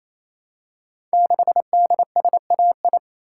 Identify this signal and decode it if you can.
Morse code, heard as a keyed tone, about 27 wpm, 700 Hz. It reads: 6DSAI
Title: Code Groups - 5 Letter:Numbers - 28wpm